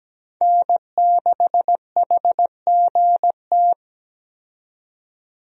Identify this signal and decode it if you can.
Morse code, heard as a keyed tone, about 17 wpm, 700 Hz. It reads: N6HGT
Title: USA Call Signs - 5 Characters - 17wpm